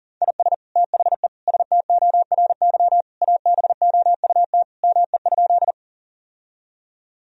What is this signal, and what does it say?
Morse code, heard as a keyed tone, about 40 wpm, 700 Hz. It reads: IS THE STORY ABOUT ME?
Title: Sentences from Top 1000 Words - Review Part 1 - 40wpm